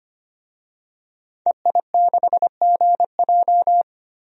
Morse code, heard as a keyed tone, about 25 wpm, 690 Hz. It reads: EI6GJ
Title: DXCC-100 Call Signs - 5 Characters - 25wpm